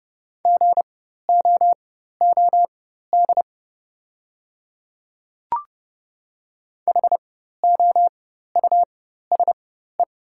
Morse code, heard as a keyed tone, about 30 wpm, spaced as for 14 wpm, 700 Hz. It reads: GOOD HOUSE
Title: 4x Character Spacing - Top 300 Words - 30wpm